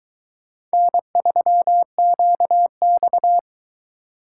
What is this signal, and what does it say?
Morse code, heard as a keyed tone, about 23 wpm, 700 Hz. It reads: N3QX